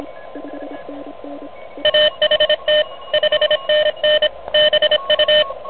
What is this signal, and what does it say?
Morse code, heard as a keyed tone, about 25 wpm, 610 Hz, with noise. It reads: AHT 5NN BU
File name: truSDX